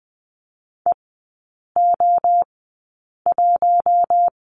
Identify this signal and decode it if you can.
Morse code, heard as a keyed tone, about 20 wpm, spaced as for 8 wpm, 700 Hz. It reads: EO1
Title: 053 Lesson 10e - Letter r - Three Characters 20wpm